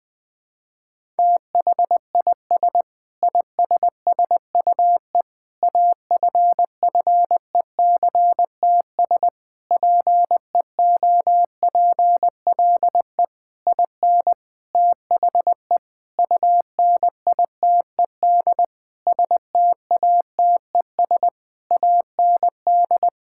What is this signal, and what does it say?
Morse code, heard as a keyed tone, about 21 wpm, 700 Hz. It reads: THIS ISSUE AFFECTS PEOPLE IN THE UNITED STATES AND
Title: Sentences with 8-Word Phrases - 20wpm